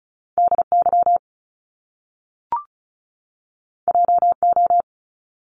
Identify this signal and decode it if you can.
Morse code, heard as a keyed tone, about 36 wpm, 700 Hz. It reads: DY JO
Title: Top Two Letter Combinations in Words 35wpm